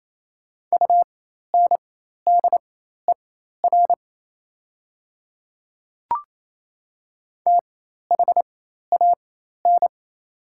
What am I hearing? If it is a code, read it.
Morse code, heard as a keyed tone, about 28 wpm, spaced as for 13 wpm, 700 Hz. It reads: UNDER THAN
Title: 4x Character Spacing - Top 500 Words - 28wpm